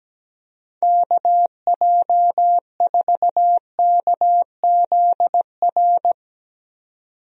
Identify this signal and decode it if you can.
Morse code, heard as a keyed tone, about 17 wpm, 700 Hz. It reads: KJ4KZR
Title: USA Call Signs - 6 Characters - 17wpm